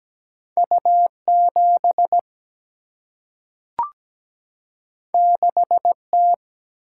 Morse code, heard as a keyed tone, about 17 wpm, 700 Hz. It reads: U7 6T
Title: Single-Letter Single-Number 17wpm